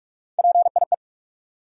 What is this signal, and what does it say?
Morse code, heard as a keyed tone, about 43 wpm, 700 Hz. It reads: PIE
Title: Three Letter Words - 45wpm